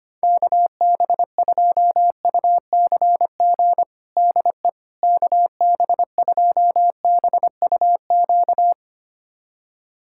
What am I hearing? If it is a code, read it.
Morse code, heard as a keyed tone, about 25 wpm, 700 Hz. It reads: KB2UCG DE KB2BUQ
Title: USA Callsign DE Callsign - 6 Characters - 25wpm